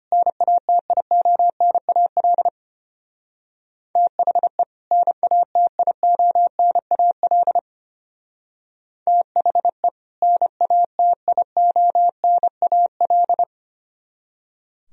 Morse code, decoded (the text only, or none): NATIONAL THE NATIONAL THE NATIONAL